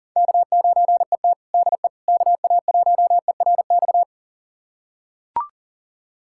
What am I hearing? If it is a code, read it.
Morse code, heard as a keyed tone, about 40 wpm, 700 Hz. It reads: K9ET DE KA1ERX
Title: USA Callsign DE Callsign 40wpm